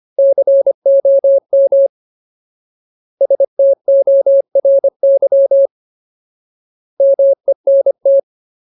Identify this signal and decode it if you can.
Morse code, heard as a keyed tone, about 25 wpm, 550 Hz. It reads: COM STORY MENT
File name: Ultimate Word List / 25WPM Ultimate